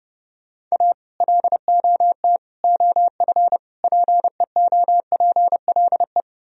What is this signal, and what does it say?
Morse code, heard as a keyed tone, about 40 wpm, 700 Hz. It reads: A LOT OF PEOPLE